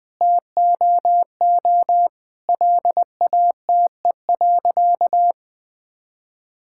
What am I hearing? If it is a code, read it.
Morse code, heard as a keyed tone, about 20 wpm, 700 Hz. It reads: TOO LATE.
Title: Sentences with 7-Word Phrases - 20wpm